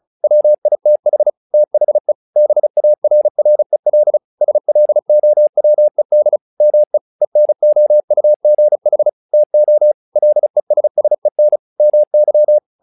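Morse code, heard as a keyed tone, about 34 wpm, 600 Hz. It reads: WITH THE BARREL SLOWED ME ENOUGH TO LESSEN MY
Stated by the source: Morse practice text "Tower Trouble" sent at 35 WPM, from the SKCC - Straight Key Century Club learning center audio files.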